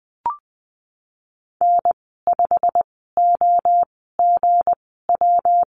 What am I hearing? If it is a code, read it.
Morse code, heard as a keyed tone, about 20 wpm, spaced as for 14 wpm, 700 Hz. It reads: N5OGW
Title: 245 Lesson 39g - Letter z - Call Signs 20wpm